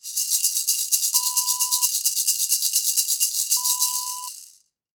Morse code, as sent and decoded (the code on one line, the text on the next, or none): none